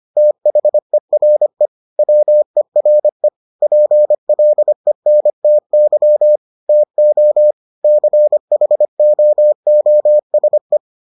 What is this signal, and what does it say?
Morse code, heard as a keyed tone, about 25 wpm, 600 Hz. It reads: THERE WERE PLENTY TO CHOOSE